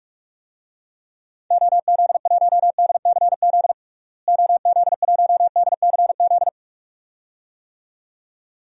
Morse code, heard as a keyed tone, about 41 wpm, 700 Hz. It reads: OZ1DCZ OZ1DCZ
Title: DXCC-100 Call Signs - 6 Characters - Repeated 2x - 45wpm